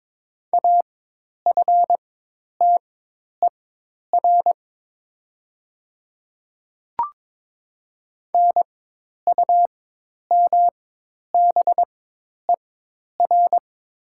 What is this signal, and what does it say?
Morse code, heard as a keyed tone, about 23 wpm, spaced as for 10 wpm, 700 Hz. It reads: AFTER NUMBER
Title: 4x Character Spacing - Top 200 Words - 22wpm